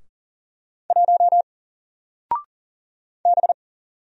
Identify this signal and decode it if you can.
Morse code, heard as a keyed tone, about 40 wpm, 700 Hz. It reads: 1 B
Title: Single Character (Letter:Number:Punctuation:Prosign) Alt 40wpm